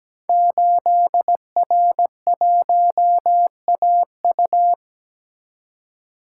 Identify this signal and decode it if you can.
Morse code, heard as a keyed tone, about 17 wpm, 700 Hz. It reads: 8R1AU
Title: Code Groups - 5 Letter:Numbers - 17wpm